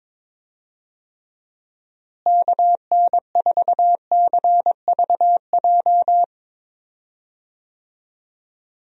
Morse code, heard as a keyed tone, about 21 wpm, 700 Hz. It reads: KN4CVJ
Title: USA Call Signs - 6 Characters - Repeated 2x - 22wpm